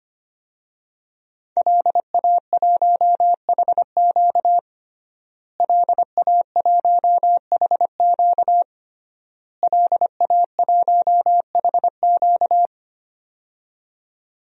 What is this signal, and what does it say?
Morse code, heard as a keyed tone, about 25 wpm, 700 Hz. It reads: LA1HQ LA1HQ LA1HQ
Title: DXCC-100 Call Signs - 5 Characters - Repeated 3x - 25wpm